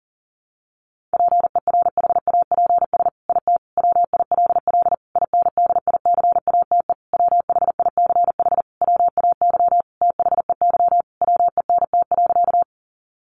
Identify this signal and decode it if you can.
Morse code, heard as a keyed tone, about 40 wpm, 700 Hz. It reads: PERHAPS IT WILL INDICATE WHICH WAY THEY WENT.
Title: Sentences from Top 1300 Words 40wpm